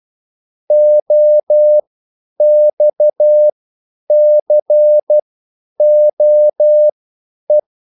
Morse code, heard as a keyed tone, about 12 wpm, spaced as for 9 wpm, 600 Hz. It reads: OXCOE